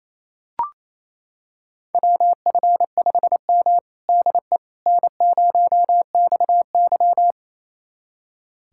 Morse code, heard as a keyed tone, about 28 wpm, 700 Hz. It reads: WF5M DE N0XY
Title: USA Callsign DE Callsign - 4 Characters - 28wpm